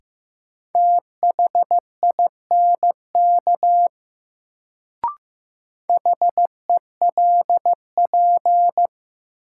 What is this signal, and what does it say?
Morse code, heard as a keyed tone, about 15 wpm, 700 Hz. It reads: THINK HELP
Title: Sets of 2 Words - New Word Drill - 15wpm